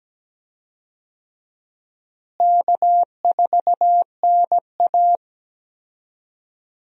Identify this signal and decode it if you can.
Morse code, heard as a keyed tone, about 17 wpm, 700 Hz. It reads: K4NA
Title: USA Call Signs - 4 Characters - Repeated 2x - 17wpm